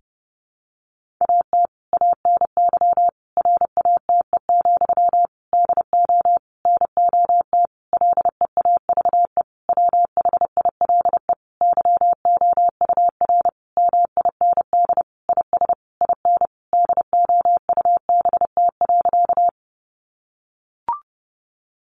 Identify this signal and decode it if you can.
Morse code, heard as a keyed tone, about 30 wpm, 700 Hz. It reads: AT ANY RATE, DO NOT LEAVE WHILE YOUR MIND IS IN DOUBT.
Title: Sentences from Top 1500 Words 30wpm